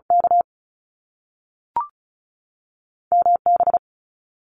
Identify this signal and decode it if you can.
Morse code, heard as a keyed tone, about 36 wpm, 700 Hz. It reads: K M B
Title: Canadian Provinces and Territories 35wpm